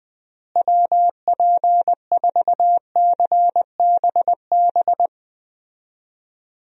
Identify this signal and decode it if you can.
Morse code, heard as a keyed tone, about 20 wpm, 700 Hz. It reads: WP4CBB
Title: USA Call Signs - 6 Characters - Repeated 3x - 20wpm